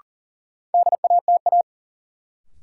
Morse code, heard as a keyed tone, about 40 wpm, 700 Hz. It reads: DATA
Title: Sentences from Top 1000 Words - Review Part 3 - New Word Drill - 40wpm